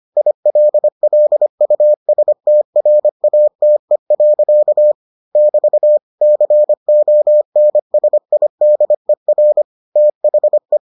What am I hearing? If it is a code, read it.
Morse code, heard as a keyed tone, about 25 wpm, 600 Hz. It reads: ILLUSTRATE. <BT> CONSIDER THE